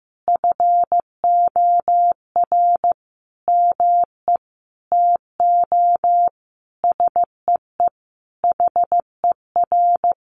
Morse code, heard as a keyed tone, about 15 wpm, 700 Hz. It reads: FOR ME TO SEE HER
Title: Sentences from Top 500 Words 15wpm